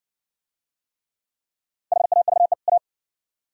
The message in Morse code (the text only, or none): HAVE A